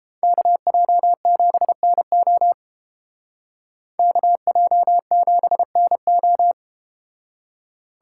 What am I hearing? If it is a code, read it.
Morse code, heard as a keyed tone, about 31 wpm, 700 Hz. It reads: KJ7NO KJ7NO